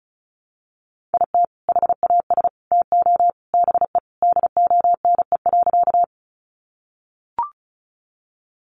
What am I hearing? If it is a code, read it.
Morse code, heard as a keyed tone, about 34 wpm, 700 Hz. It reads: IT HAS TO BE DONE.
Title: Sentences from Top 500 Words 35wpm